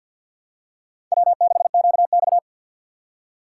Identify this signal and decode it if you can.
Morse code, heard as a keyed tone, about 49 wpm, 700 Hz. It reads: W6QX
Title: USA Call Signs - 4 Characters - 3x Character Spacing - 50wpm